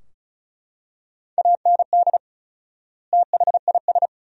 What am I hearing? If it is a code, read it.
Morse code, heard as a keyed tone, about 36 wpm, 700 Hz. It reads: AND THIS
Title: 4x Word-Spacing - Sentences from Top 300 Words - 35wpm